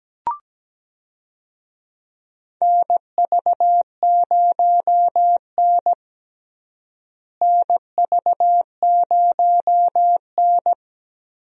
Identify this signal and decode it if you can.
Morse code, heard as a keyed tone, about 17 wpm, 700 Hz. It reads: NV0N NV0N
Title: USA Call Signs - 4 Characters - Repeated 3x - 17wpm